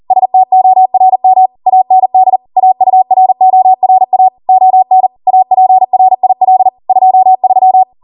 Morse code, heard as a keyed tone, about 40 wpm, 750 Hz. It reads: STORM AND AURORA ON APRIL 23